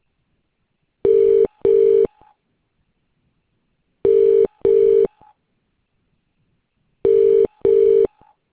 Ringback tone (UK), 400 + 450 Hz.